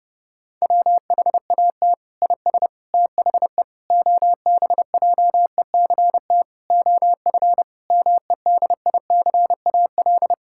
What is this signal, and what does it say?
Morse code, heard as a keyed tone, about 30 wpm, 700 Hz. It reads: WHAT IS THE OBJECT OF MEDICAL
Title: Sentences from Top 1000 Words - Review Part 6 - 30wpm